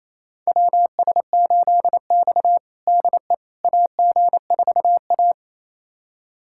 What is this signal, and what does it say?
Morse code, heard as a keyed tone, about 28 wpm, 700 Hz. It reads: WS8X DE AG4A
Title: USA Callsign DE Callsign - 4 Characters - 28wpm